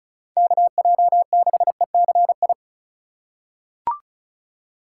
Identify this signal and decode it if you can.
Morse code, heard as a keyed tone, about 36 wpm, 700 Hz. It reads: KJ6ECI
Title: USA Call Signs - 8x Character Spacing - 35wpm